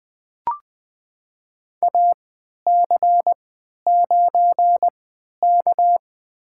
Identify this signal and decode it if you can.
Morse code, heard as a keyed tone, about 20 wpm, spaced as for 11 wpm, 700 Hz. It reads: AC9K
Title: USA Call Signs - 3x Character Spacing - 20wpm